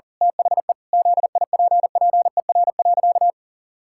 Morse code, heard as a keyed tone, about 40 wpm, 700 Hz. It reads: THE ZIPPER.
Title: Pangram Challenge v2 - 40wpm